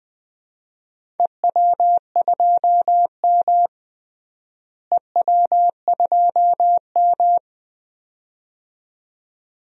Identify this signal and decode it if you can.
Morse code, heard as a keyed tone, about 21 wpm, 700 Hz. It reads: EW2M EW2M
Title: DXCC-100 Call Signs - 4 Characters - Repeated 2x - 20wpm